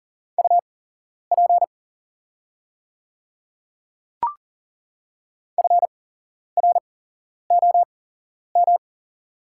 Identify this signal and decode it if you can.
Morse code, heard as a keyed tone, about 40 wpm, spaced as for 11 wpm, 690 Hz. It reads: UP FROM
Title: 8x Character Spacing - Top 100 Words - 40wpm